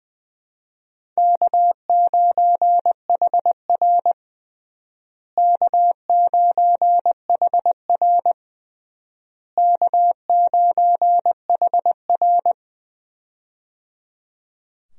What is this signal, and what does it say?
Morse code, heard as a keyed tone, about 21 wpm, 700 Hz. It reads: K9HR K9HR K9HR
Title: USA Call Signs - 4 Characters - Repeated 3x - 20wpm